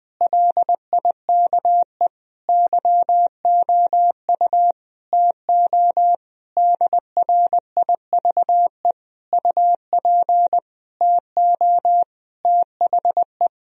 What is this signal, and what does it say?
Morse code, heard as a keyed tone, about 20 wpm, 700 Hz. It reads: LIKE YOU TO DRIVE UP TO THE
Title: Sentences from Top 1000 Words - Review Part 2 - 20wpm